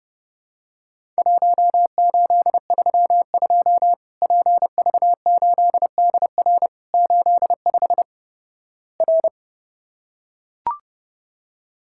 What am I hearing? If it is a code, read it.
Morse code, heard as a keyed tone, about 32 wpm, 700 Hz. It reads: 1832 PV8DR 85 R
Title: Worked All Europe DX Contest - QTC Exchange - 30wpm